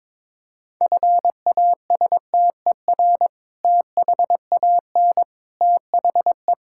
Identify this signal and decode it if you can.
Morse code, heard as a keyed tone, about 21 wpm, 700 Hz. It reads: FASTER THAN THE
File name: practice / 22wpm / Sentences with 2-Word Phrases - 22wpm